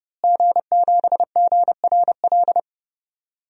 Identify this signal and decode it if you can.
Morse code, heard as a keyed tone, about 30 wpm, 700 Hz. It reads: G7GRL
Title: DXCC-100 Call Signs - 5 Characters - Repeated 2x - 30wpm